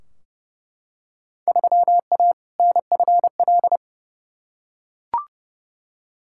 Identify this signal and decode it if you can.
Morse code, heard as a keyed tone, about 30 wpm, 700 Hz. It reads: 3A NFL
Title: Field Day Exchanges - 30wpm